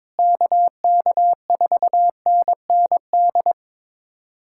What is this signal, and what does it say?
Morse code, heard as a keyed tone, about 22 wpm, 700 Hz. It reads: KK4NND
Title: USA Callsign DE Callsign 22wpm